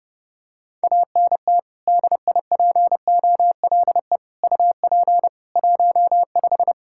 Morse code, heard as a keyed tone, about 32 wpm, 700 Hz. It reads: ANT DIPOLE UP 15
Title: K1USN SST QSO Practice 30wpm